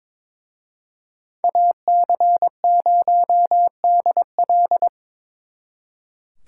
Morse code, heard as a keyed tone, about 26 wpm, 700 Hz. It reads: AC0DL
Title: Speed Racing - USA Call Signs 22wpm